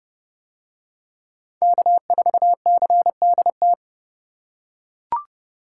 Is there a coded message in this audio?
Yes: Morse code from 1.6 to 3.7 s, heard as a keyed tone, about 29 wpm, 700 Hz. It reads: K4CDT